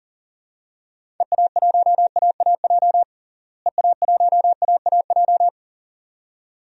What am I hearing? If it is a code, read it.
Morse code, heard as a keyed tone, about 40 wpm, 700 Hz. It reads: EA1AAJ EA1AAJ
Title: DXCC-100 Call Signs - 6 Characters - Repeated 2x - 40wpm